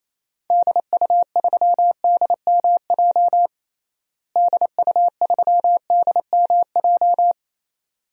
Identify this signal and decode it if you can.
Morse code, heard as a keyed tone, about 28 wpm, 700 Hz. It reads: DU3DMJ DU3DMJ
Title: DXCC-100 Call Signs - 6 Characters - Repeated 2x - 28wpm